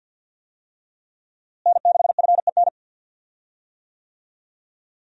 Morse code, heard as a keyed tone, about 50 wpm, 700 Hz. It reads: N6FEN
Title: USA Call Signs - 5 Characters - Repeated 3x - 50wpm